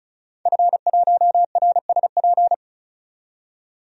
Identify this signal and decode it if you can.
Morse code, heard as a keyed tone, about 34 wpm, 700 Hz. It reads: F1RSP
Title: DXCC-100 Call Signs - 5 Characters - 35wpm